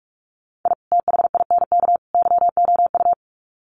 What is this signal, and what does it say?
Morse code, heard as a keyed tone, about 44 wpm, 700 Hz. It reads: I THINK YOU
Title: Sets of 3 Words 45wpm